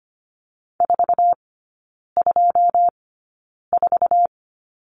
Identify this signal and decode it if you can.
Morse code, heard as a keyed tone, about 25 wpm, spaced as for 9 wpm, 700 Hz. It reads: 424
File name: Lesson 14e - Number 2 - Three Characters 25wpm